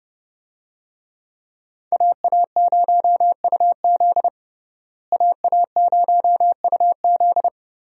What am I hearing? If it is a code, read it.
Morse code, heard as a keyed tone, about 30 wpm, 700 Hz. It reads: AA0UZ AA0UZ
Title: USA Call Signs - 5 Characters - Repeated 2x - 30wpm